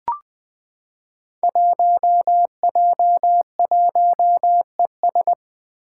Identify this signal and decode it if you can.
Morse code, heard as a keyed tone, about 20 wpm, 700 Hz. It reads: 1J1ES
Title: Five Challenging Letter-Number Combinations 20wpm